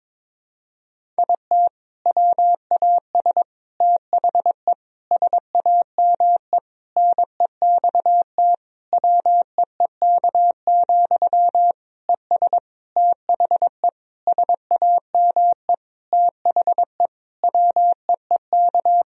Morse code, heard as a keyed tone, about 21 wpm, 700 Hz. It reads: IT WAS THE SAME NEXT WEEK, ES THE SAME THE WEEK
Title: Sentences from Top 500 Words and Abbreviations 22wpm